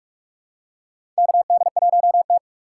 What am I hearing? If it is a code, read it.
Morse code, heard as a keyed tone, about 46 wpm, 700 Hz. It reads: KD1T
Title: USA Call Signs - 4 Characters - 4x Character Spacing - 45wpm